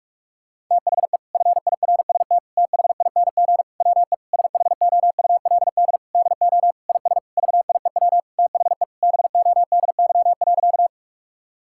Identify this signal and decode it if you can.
Morse code, heard as a keyed tone, about 48 wpm, 700 Hz. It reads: THE FIRST THING WE SHOULD DO IS VIEW THE BODY.
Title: Sentences from Top 1000 Words - Review Part 6 - 45wpm